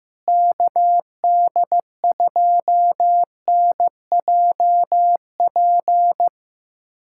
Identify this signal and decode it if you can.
Morse code, heard as a keyed tone, about 15 wpm, 700 Hz. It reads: KD2NJP
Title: USA Callsign DE Callsign - 6 Characters - 15wpm